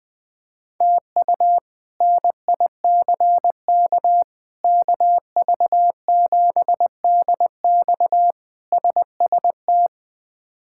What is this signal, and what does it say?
Morse code, heard as a keyed tone, about 21 wpm, 700 Hz. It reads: TU NICK KV7DX SST
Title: K1USN SST QSO Practice - 20wpm